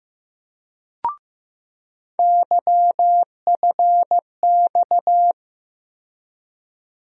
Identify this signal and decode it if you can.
Morse code, heard as a keyed tone, about 15 wpm, 700 Hz. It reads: YFX
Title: USA Call Sign Postfixes - 3 Letters 15wpm